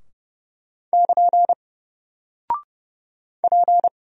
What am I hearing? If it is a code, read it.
Morse code, heard as a keyed tone, about 32 wpm, 700 Hz. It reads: <KN> P
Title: Single Character (Letter:Number:Punctuation:Prosign) 30wpm